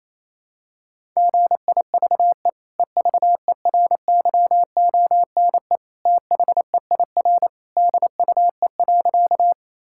Morse code, heard as a keyed tone, about 28 wpm, 700 Hz. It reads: GIVE EVERYONE THEIR DUE.
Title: Sentences from Top 1000 Words - Review Part 6 - 28wpm